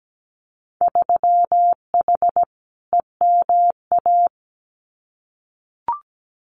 Morse code, heard as a keyed tone, about 17 wpm, 700 Hz. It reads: 3H EMA
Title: Winter Field Day Exchanges Encore 17wpm